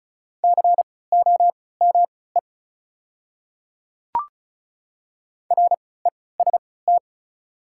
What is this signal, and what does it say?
Morse code, heard as a keyed tone, about 36 wpm, spaced as for 20 wpm, 690 Hz. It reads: COME REST
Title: 3x Character Spacing - Top 500 Words - 35wpm